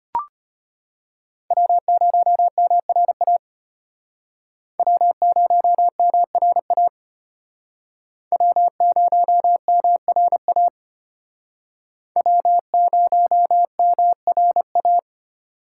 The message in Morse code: W0MRA W0MRA W0MRA W0MRA

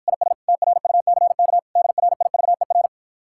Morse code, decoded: IS TRUCK DRIVER

